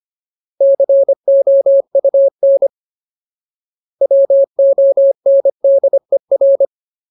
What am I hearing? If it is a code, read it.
Morse code, heard as a keyed tone, about 25 wpm, 550 Hz. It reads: COUN WONDER
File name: Ultimate Word List / 25WPM Ultimate